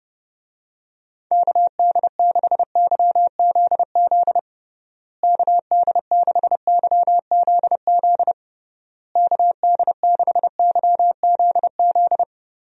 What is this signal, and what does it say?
Morse code, heard as a keyed tone, about 31 wpm, 700 Hz. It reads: KD6YZZ KD6YZZ KD6YZZ
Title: USA Call Signs - 6 Characters - Repeated 3x - 30wpm